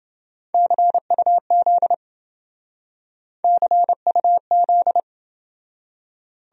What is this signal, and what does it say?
Morse code, heard as a keyed tone, about 27 wpm, 700 Hz. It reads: CUZ CUZ